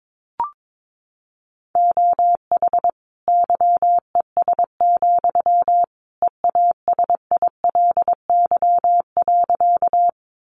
Morse code, heard as a keyed tone, about 22 wpm, 700 Hz. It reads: OH YES, EASILY.
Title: Sentences from Top 900 Words 22wpm